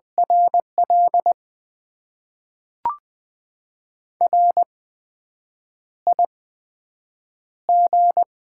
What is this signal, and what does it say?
Morse code, heard as a keyed tone, about 20 wpm, spaced as for 7 wpm, 700 Hz. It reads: RL RIG
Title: Top 20 QSO Elements - 8x Character Spacing - 20wpm